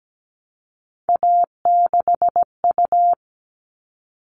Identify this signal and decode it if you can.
Morse code, heard as a keyed tone, about 17 wpm, 700 Hz. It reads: A6U
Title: All Cut Numbers - 3-Digit 17wpm